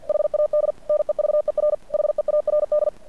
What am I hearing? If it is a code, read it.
Morse code, heard as a keyed tone, about 50 wpm, 590 Hz. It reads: HAD NEVER HEARD
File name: archived / CRSnet